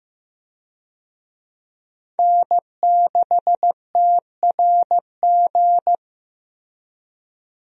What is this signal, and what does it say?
Morse code, heard as a keyed tone, about 16 wpm, 700 Hz. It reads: N6TRG
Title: USA Call Signs - 5 Characters - Repeated 3x - 15wpm